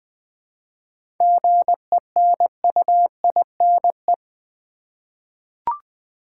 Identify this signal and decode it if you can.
Morse code, heard as a keyed tone, about 20 wpm, 700 Hz. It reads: GENUINE
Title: 7 Letter Words - First 3 Letters Given - 20wpm